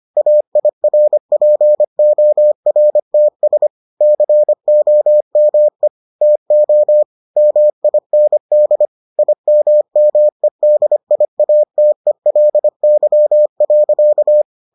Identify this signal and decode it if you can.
Morse code, heard as a keyed tone, about 25 wpm, 600 Hz. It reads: AIRPORTS COME TO MIND IMMEDIATELY.